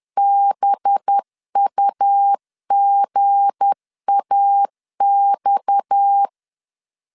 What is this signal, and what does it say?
Morse code, heard as a keyed tone, about 10 wpm, 800 Hz. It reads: BUGAX
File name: LEZIONI / 8 YJ45 / 8 YJ45 10wpm 2549